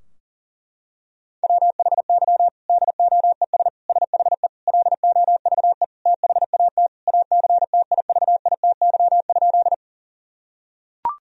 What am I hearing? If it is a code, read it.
Morse code, heard as a keyed tone, about 40 wpm, 700 Hz. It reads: WHY DOES SHE LOVE THAT ACTIVITY?
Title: Sentences from Top 1000 Words - Review Part 4 - 40wpm